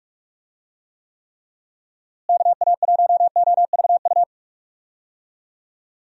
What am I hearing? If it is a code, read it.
Morse code, heard as a keyed tone, about 45 wpm, 700 Hz. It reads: KA1OVU
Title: USA Call Signs - 6 Characters - Repeated 3x - 45wpm